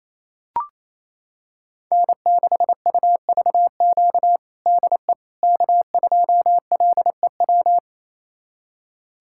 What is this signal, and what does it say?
Morse code, heard as a keyed tone, about 28 wpm, 700 Hz. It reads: N6UVQ DE K2LEW
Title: USA Callsign DE Callsign - 5 Characters - 28wpm